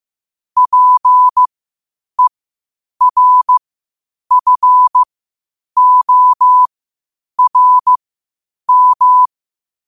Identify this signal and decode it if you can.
Morse code, heard as a keyed tone, about 15 wpm, spaced as for 8 wpm, 1000 Hz. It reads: PERFORM
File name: qotdfiles / 8wpm